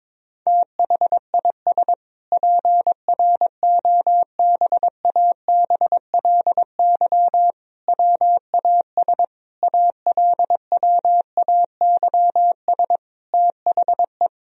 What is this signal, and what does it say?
Morse code, heard as a keyed tone, about 22 wpm, 700 Hz. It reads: THIS PROBABLY WAS ALWAYS THE
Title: Sentences from Top 1000 Words - Review Part 4 - 22wpm